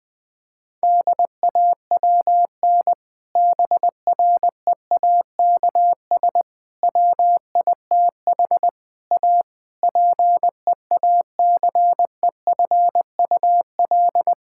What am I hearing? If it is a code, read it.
Morse code, heard as a keyed tone, about 20 wpm, 700 Hz. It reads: DAWN BREAKS WITH A PEACEFUL
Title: Difficult Sentences with Hints - 20wpm